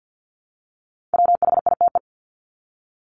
Morse code, heard as a keyed tone, about 50 wpm, 700 Hz. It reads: WHITE
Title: Top 300 Words 50wpm